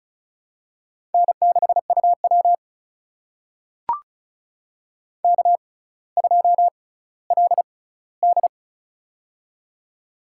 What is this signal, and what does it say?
Morse code, heard as a keyed tone, about 36 wpm, spaced as for 18 wpm, 690 Hz. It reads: N6UW K2LD
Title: USA Call Signs - 4 Characters - 6x Character Spacing - 35wpm